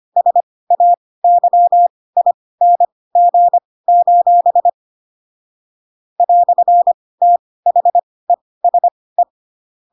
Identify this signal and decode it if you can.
Morse code, heard as a keyed tone, about 25 wpm, spaced as for 18 wpm, 700 Hz. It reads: SAYING: "THESE
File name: Pinocchio / Pinocchio ch7 12 25 ik1hge